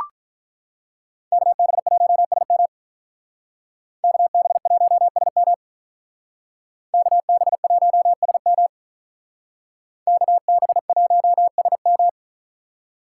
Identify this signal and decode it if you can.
Morse code, heard as a keyed tone, about 44 wpm, 700 Hz. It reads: KB1SM KB1SM KB1SM KB1SM